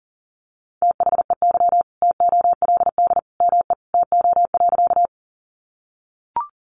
Morse code, heard as a keyed tone, about 40 wpm, 700 Hz. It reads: THEY TOLD ME TO.
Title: Sentences from Top 400 Words 40wpm